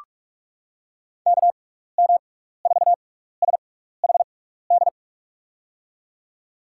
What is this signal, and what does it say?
Morse code, heard as a keyed tone, about 45 wpm, spaced as for 15 wpm, 700 Hz. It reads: KM4SHD
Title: USA Call Signs - 6 Characters - 6x Character Spacing - 45wpm